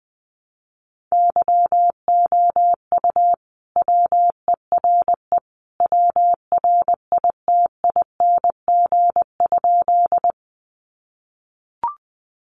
Morse code, heard as a keyed tone, about 20 wpm, 700 Hz. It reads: YOU WERE WRITING?
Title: Sentences from Top 800 Words 20wpm